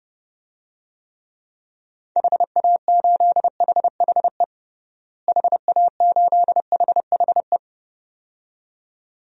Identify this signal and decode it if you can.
Morse code, heard as a keyed tone, about 29 wpm, 700 Hz. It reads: HA8HHE HA8HHE
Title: DXCC-100 Call Signs - 6 Characters - Repeated 2x - 30wpm